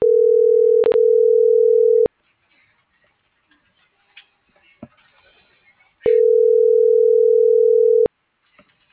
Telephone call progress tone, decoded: ringback tone

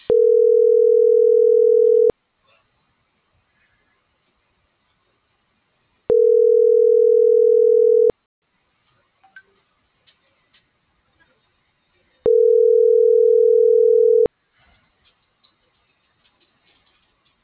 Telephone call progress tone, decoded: ringback tone

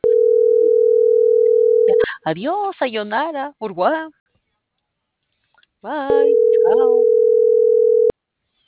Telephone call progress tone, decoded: ringback tone

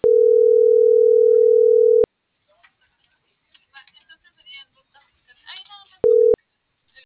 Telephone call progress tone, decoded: ringback tone

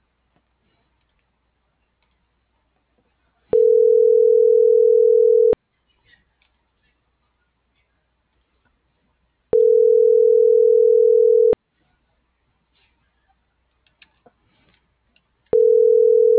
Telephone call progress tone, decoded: ringback tone

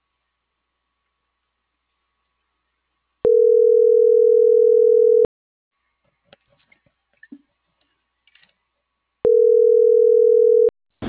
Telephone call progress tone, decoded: ringback tone

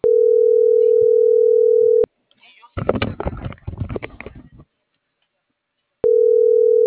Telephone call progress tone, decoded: ringback tone

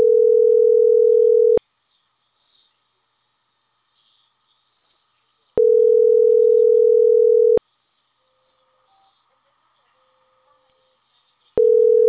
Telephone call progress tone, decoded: ringback tone

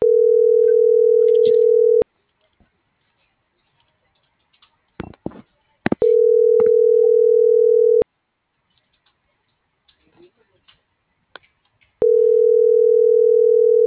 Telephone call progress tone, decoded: ringback tone

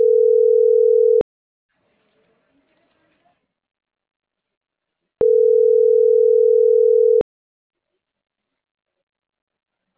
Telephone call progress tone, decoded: ringback tone

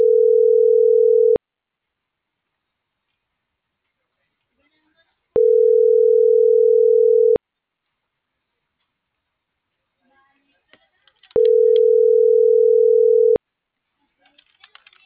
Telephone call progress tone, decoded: ringback tone

